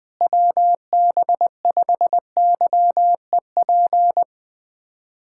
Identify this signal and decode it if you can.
Morse code, heard as a keyed tone, about 20 wpm, 700 Hz. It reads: WB5YEP
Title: USA Callsign DE Callsign - 6 Characters - 20wpm